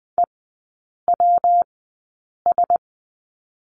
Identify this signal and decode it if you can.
Morse code, heard as a keyed tone, about 20 wpm, spaced as for 8 wpm, 700 Hz. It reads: EWS
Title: 114 Lesson 19e - Letter w - Three Characters 20wpm